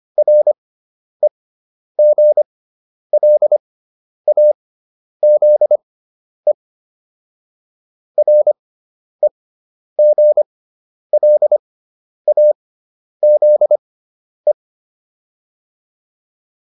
Morse code, heard as a keyed tone, about 25 wpm, spaced as for 10 wpm, 600 Hz. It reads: REGLAZE REGLAZE